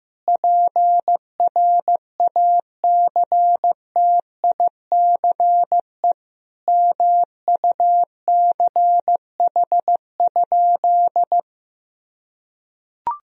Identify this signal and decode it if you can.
Morse code, heard as a keyed tone, about 15 wpm, 700 Hz. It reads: PRACTICE MUCH?
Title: Sentences from Top 1000 Words - Review Part 4 - 15wpm